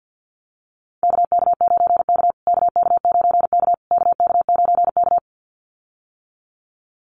Morse code, heard as a keyed tone, about 50 wpm, 700 Hz. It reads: KK9K KK9K KK9K
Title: USA Call Signs Repeated 3x 50wpm